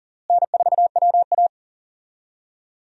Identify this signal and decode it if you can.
Morse code, heard as a keyed tone, about 40 wpm, 700 Hz. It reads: N4WA
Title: USA Call Signs - 4 Characters - Repeated 3x - 40wpm